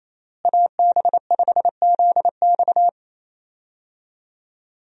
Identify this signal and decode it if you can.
Morse code, heard as a keyed tone, about 28 wpm, 700 Hz. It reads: AB5ZX
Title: USA Call Signs - 5 Characters - Repeated 3x - 28wpm